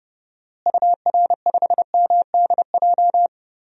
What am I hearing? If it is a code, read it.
Morse code, heard as a keyed tone, about 30 wpm, 700 Hz. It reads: UR5MDJ
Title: DXCC-100 Call Signs - 6 Characters - Repeated 3x - 30wpm